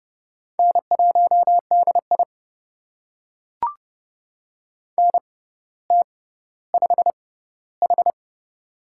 Morse code, heard as a keyed tone, about 30 wpm, spaced as for 15 wpm, 700 Hz. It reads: N1DI NT5H
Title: USA Call Signs - 4 Characters - 6x Character Spacing - 30wpm